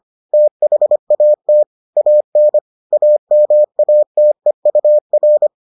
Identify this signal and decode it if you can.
Morse code, heard as a keyed tone, about 25 wpm, 600 Hz. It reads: THAT AN AMATEUR